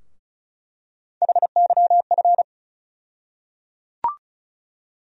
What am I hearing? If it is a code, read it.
Morse code, heard as a keyed tone, about 36 wpm, 700 Hz. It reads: HYF
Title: USA Call Sign Postfixes - 3 Letters 35wpm